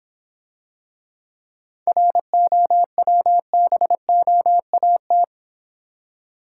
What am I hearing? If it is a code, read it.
Morse code, heard as a keyed tone, about 26 wpm, 700 Hz. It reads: ROWBOAT